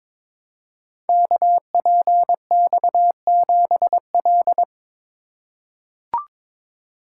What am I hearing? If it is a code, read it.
Morse code, heard as a keyed tone, about 22 wpm, 700 Hz. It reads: KPX7L
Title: Code Groups - 5 Letter:Numbers - 22wpm